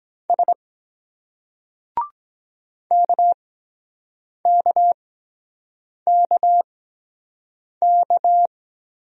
Morse code, heard as a keyed tone, about 23 wpm, spaced as for 7 wpm, 700 Hz. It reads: S KKKK